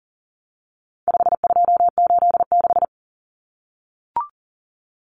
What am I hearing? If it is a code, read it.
Morse code, heard as a keyed tone, about 40 wpm, 700 Hz. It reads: 5286
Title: Numbers - 4-Digit 40wpm